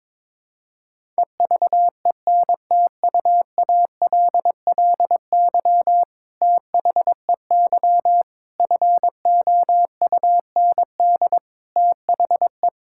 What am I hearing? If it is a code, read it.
Morse code, heard as a keyed tone, about 22 wpm, 700 Hz. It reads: EVENTUALLY THEY FOUND THE
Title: Sentences from Top 1000 Words - Review Part 6 - 22wpm